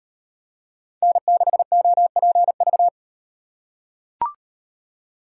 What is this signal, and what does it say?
Morse code, heard as a keyed tone, about 38 wpm, 700 Hz. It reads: N6OPV